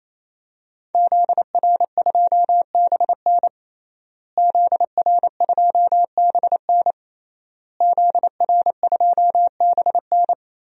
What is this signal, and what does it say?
Morse code, heard as a keyed tone, about 28 wpm, 700 Hz. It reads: ZR2BN ZR2BN ZR2BN
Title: DXCC-100 Call Signs - 5 Characters - Repeated 3x - 28wpm